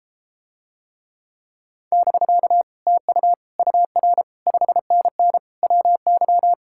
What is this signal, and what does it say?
Morse code, heard as a keyed tone, about 33 wpm, 700 Hz. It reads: <BK> TU UR 5NN WY